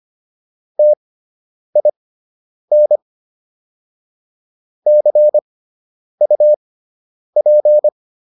Morse code, heard as a keyed tone, about 25 wpm, spaced as for 9 wpm, 600 Hz. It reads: TIN CUP